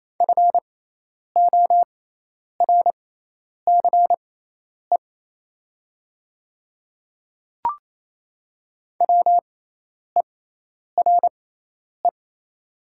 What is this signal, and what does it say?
Morse code, heard as a keyed tone, about 27 wpm, spaced as for 10 wpm, 700 Hz. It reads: FORCE WERE
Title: 6x Character Spacing - Top 500 Words - 28wpm